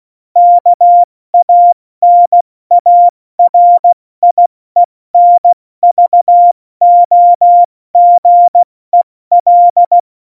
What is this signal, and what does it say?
Morse code, heard as a keyed tone, about 16 wpm, spaced as for 14 wpm, 700 Hz. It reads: KANARIENVOGEL
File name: Morsekurs / Witze / Witz 21. August 2018, 70 F80